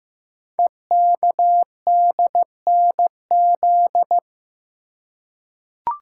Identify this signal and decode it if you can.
Morse code, heard as a keyed tone, about 15 wpm, 700 Hz. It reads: EKDNZ
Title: Code Groups - 5 Letters - 15wpm